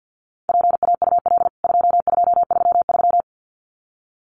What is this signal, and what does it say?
Morse code, heard as a keyed tone, about 50 wpm, 700 Hz. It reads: PAUL 2233
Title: CWT Exchange 50wpm